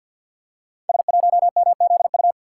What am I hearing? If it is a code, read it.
Morse code, heard as a keyed tone, about 51 wpm, 700 Hz. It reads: S1MZU
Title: Code Groups - 5 Letter:Numbers - 50wpm